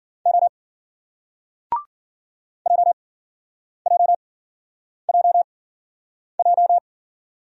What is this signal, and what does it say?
Morse code, heard as a keyed tone, about 54 wpm, spaced as for 9 wpm, 700 Hz. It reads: O JJJJ